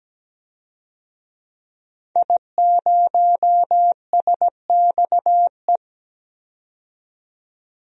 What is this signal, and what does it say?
Morse code, heard as a keyed tone, about 17 wpm, 700 Hz. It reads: I0SXE
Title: DXCC-100 Call Signs - 5 Characters - 17wpm